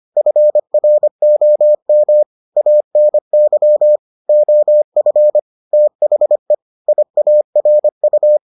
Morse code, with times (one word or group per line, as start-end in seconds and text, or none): 0.2-2.2 FROM
2.6-4.0 ANY
4.3-5.4 OF
5.7-6.5 THE
6.9-8.4 IARU